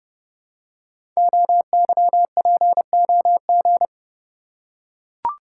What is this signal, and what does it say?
Morse code, heard as a keyed tone, about 30 wpm, 700 Hz. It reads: OYPOG
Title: Code Groups - 5 Letters - 30wpm